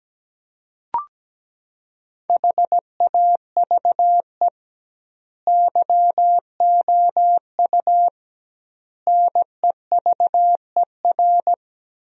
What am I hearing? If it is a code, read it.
Morse code, heard as a keyed tone, about 17 wpm, 700 Hz. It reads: HAVE YOU NEVER
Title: 2x Word-Spacing - Sentences from Top 300 Words - 17wpm